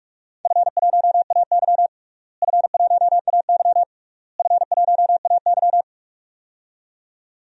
Morse code, heard as a keyed tone, about 46 wpm, 700 Hz. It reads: F1AY F1AY F1AY
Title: DXCC-100 Call Signs - 4 Characters - Repeated 3x - 45wpm